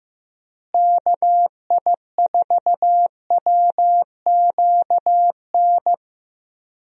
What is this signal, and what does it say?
Morse code, heard as a keyed tone, about 15 wpm, 700 Hz. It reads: KI4WQN
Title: USA Call Signs - 6 Characters - Repeated 2x - 15wpm